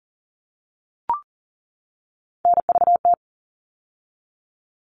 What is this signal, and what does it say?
Morse code, heard as a keyed tone, about 40 wpm, 700 Hz. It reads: NVT
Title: All Cut Numbers - 3-Digit 40wpm